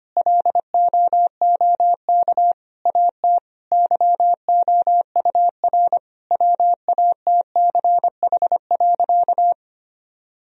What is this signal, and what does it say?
Morse code, heard as a keyed tone, about 25 wpm, 700 Hz. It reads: LOOK AT YOUR WATCH.
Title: Sentences from Top 1000 Words - Review Part 1 - 25wpm